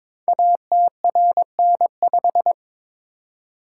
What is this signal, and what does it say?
Morse code, heard as a keyed tone, about 22 wpm, 700 Hz. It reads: ATRN5
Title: Code Groups - 5 Letter:Numbers - 22wpm